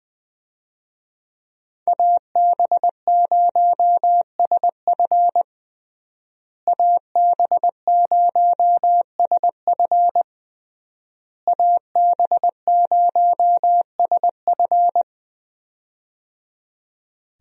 Morse code, heard as a keyed tone, about 20 wpm, 700 Hz. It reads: AB0SF AB0SF AB0SF
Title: USA Call Signs - 5 Characters - Repeated 3x - 20wpm